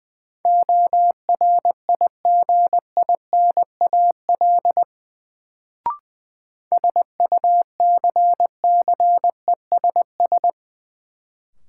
Morse code, heard as a keyed tone, about 21 wpm, 700 Hz. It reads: ORIGINAL SUCCESS
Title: Sentences from Top 900 Words - New Word Drill - 20wpm